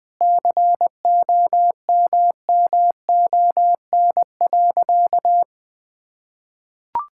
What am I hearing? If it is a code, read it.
Morse code, heard as a keyed tone, about 21 wpm, 700 Hz. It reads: COMMON.
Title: Sentences with 6-Word Phrases - Spoken Beginning - 20wpm